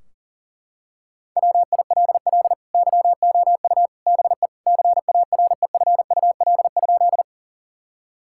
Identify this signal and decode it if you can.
Morse code, heard as a keyed tone, about 40 wpm, 700 Hz. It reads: WILL YOU BE CAREFUL?
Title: Sentences from Top 1600 Words 40wpm